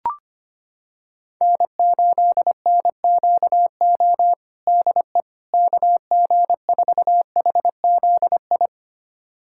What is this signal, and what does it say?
Morse code, heard as a keyed tone, about 25 wpm, 700 Hz. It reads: N8NQO DE KG4HZI
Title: USA Callsign DE Callsign 25wpm